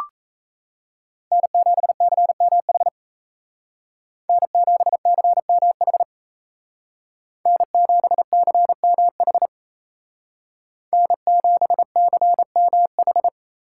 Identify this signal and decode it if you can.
Morse code, heard as a keyed tone, about 37 wpm, 700 Hz. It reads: N7CMH N7CMH N7CMH N7CMH